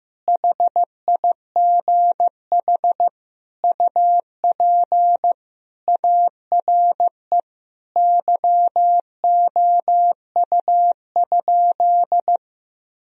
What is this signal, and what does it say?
Morse code, heard as a keyed tone, about 15 wpm, 700 Hz. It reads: HIGH UP ARE YOU?
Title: Sentences from Top 1000 Words - Review Part 1 - 15wpm